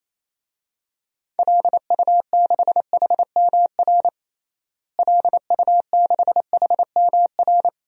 Morse code, heard as a keyed tone, about 28 wpm, 700 Hz. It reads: LU6HMR LU6HMR
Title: DXCC-100 Call Signs - 6 Characters - Repeated 3x - 28wpm